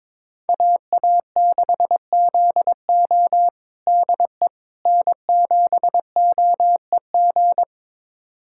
Morse code, heard as a keyed tone, about 22 wpm, 700 Hz. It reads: AA6ZO DE N7OEG
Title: USA Callsign DE Callsign - 5 Characters - 22wpm